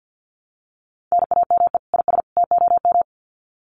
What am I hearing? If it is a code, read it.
Morse code, heard as a keyed tone, about 50 wpm, 700 Hz. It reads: NAME IS TOM
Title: Common QSO Phrases 50wpm